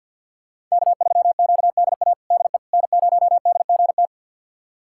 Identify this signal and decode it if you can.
Morse code, heard as a keyed tone, about 51 wpm, 700 Hz. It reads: K3QDA DE N0DGT
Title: USA Callsign DE Callsign - 5 Characters - 50wpm